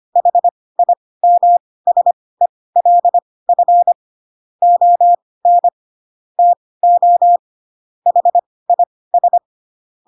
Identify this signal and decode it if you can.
Morse code, heard as a keyed tone, about 25 wpm, spaced as for 18 wpm, 700 Hz. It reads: HIMSELF ON TO HIS
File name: Pinocchio / Pinocchio ch2 12 25 ik1hge